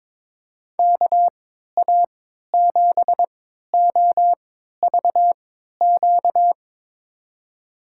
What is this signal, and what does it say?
Morse code, heard as a keyed tone, about 22 wpm, spaced as for 12 wpm, 700 Hz. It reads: KA7OVQ
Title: USA Call Signs - 6 Characters - 3x Character Spacing - 22wpm